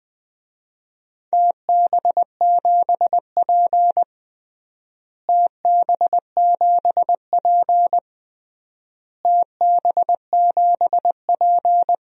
Morse code, heard as a keyed tone, about 21 wpm, 700 Hz. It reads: TB7P TB7P TB7P
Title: DXCC-100 Call Signs - 4 Characters - Repeated 3x - 20wpm